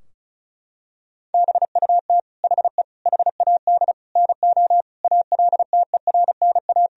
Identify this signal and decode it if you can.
Morse code, heard as a keyed tone, about 36 wpm, 700 Hz. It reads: BUT HE HAD NO ALTERNA
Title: Sentences from Top 1000 Words - Review Part 4 - 35wpm